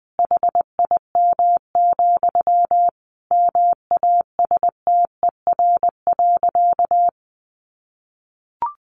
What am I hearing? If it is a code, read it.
Morse code, heard as a keyed tone, about 20 wpm, 700 Hz. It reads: HIM, MASTER.
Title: Sentences from Top 1300 Words 20wpm